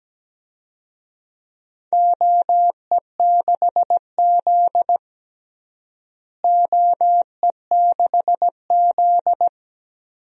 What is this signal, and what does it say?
Morse code, heard as a keyed tone, about 17 wpm, 700 Hz. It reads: OE6Z OE6Z
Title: DXCC-100 Call Signs - 4 Characters - Repeated 2x - 17wpm